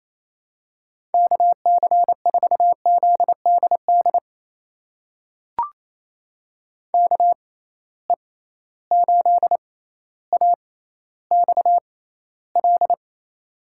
Morse code, heard as a keyed tone, about 28 wpm, spaced as for 14 wpm, 700 Hz. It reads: KC4ZDD KE8AXL
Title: USA Call Signs - 6 Characters - 6x Character Spacing - 28wpm